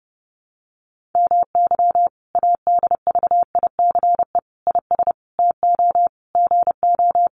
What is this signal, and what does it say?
Morse code, heard as a keyed tone, about 30 wpm, 700 Hz. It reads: MY ADVICE IS TO GO
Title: Sentences from Top 1300 Words 30wpm